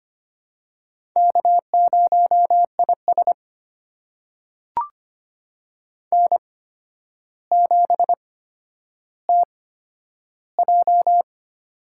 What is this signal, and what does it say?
Morse code, heard as a keyed tone, about 25 wpm, spaced as for 11 wpm, 700 Hz. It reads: K0IS N7TJ
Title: USA Call Signs - 4 Characters - 8x Character Spacing - 25wpm